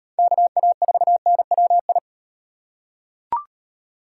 Morse code, heard as a keyed tone, about 38 wpm, 700 Hz. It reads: KA4NWI